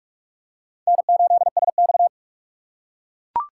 Morse code, heard as a keyed tone, about 42 wpm, 700 Hz. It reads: N8SX